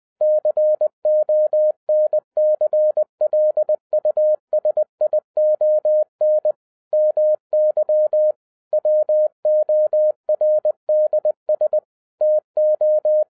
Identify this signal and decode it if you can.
Morse code, heard as a keyed tone, about 20 wpm, 610 Hz. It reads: CONCLUSION MY WORDS TO
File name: 2011-10-mp3 / n13-TIPS FROM MY 34 YEARS OF EXPERIENCE 20wpm